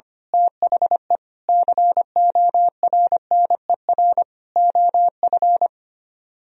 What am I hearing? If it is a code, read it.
Morse code, heard as a keyed tone, about 24 wpm, 700 Hz. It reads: THE CORNER OF
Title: Speed-Racing - Sets of 5 Words 25wpm